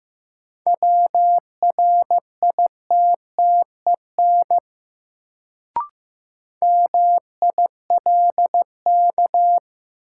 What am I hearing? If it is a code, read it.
Morse code, heard as a keyed tone, about 15 wpm, 700 Hz. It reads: WRITTEN MILK
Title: Sentences from Top 1500 Words - New Word Drill - 15wpm